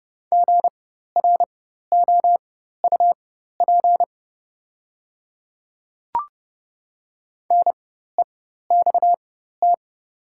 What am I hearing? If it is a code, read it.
Morse code, heard as a keyed tone, about 30 wpm, spaced as for 14 wpm, 700 Hz. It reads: GROUP NEXT
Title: 4x Character Spacing - Top 500 Words - 30wpm